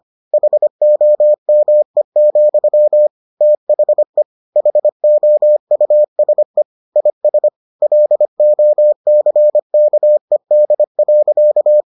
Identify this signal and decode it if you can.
Morse code, heard as a keyed tone, about 25 wpm, 600 Hz. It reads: HOME, THE HOUSE IS LOCKED.